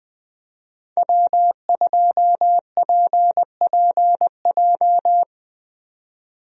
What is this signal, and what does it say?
Morse code, heard as a keyed tone, about 20 wpm, 700 Hz. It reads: W2PPJ
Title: USA Call Signs - 5 Characters - 20wpm